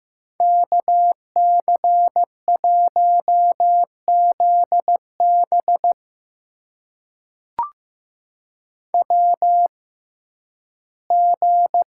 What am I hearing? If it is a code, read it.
Morse code, heard as a keyed tone, about 15 wpm, spaced as for 11 wpm, 700 Hz. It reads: KC1ZB WG
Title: USA Call Signs - 6x Character Spacing - 15wpm